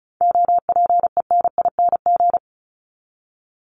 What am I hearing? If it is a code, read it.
Morse code, heard as a keyed tone, about 34 wpm, 700 Hz. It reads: OPENING
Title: Top 1751-2000 Words 35wpm